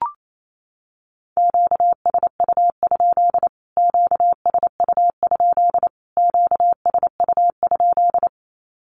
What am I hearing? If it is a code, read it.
Morse code, heard as a keyed tone, about 28 wpm, 700 Hz. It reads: QSU? QSU? QSU?
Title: Q-Codes Repeated 3x 28wpm